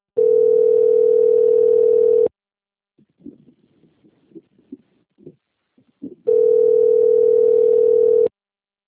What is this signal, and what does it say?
Ringback tone (North America), 440 + 480 Hz.